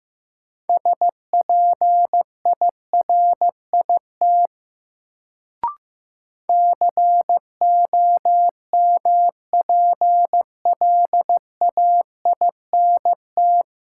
Morse code, heard as a keyed tone, about 15 wpm, 700 Hz. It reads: SPIRIT COMPLAINT
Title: Sentences from Top 1400 Words - New Word Drill - 15wpm